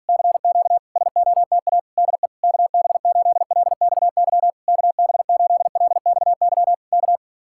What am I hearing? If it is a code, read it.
Morse code, heard as a keyed tone, about 46 wpm, 700 Hz. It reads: CQ SOTA DE KB8LXY KB8LXY K